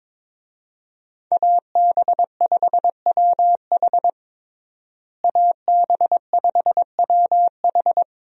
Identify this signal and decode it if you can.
Morse code, heard as a keyed tone, about 22 wpm, 700 Hz. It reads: AB5WH AB5WH
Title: USA Call Signs - 5 Characters - Repeated 2x - 22wpm